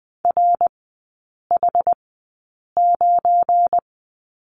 Morse code, heard as a keyed tone, about 20 wpm, spaced as for 8 wpm, 700 Hz. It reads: RH9
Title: 172 Lesson 28e - Number 9 - Three Characters 20wpm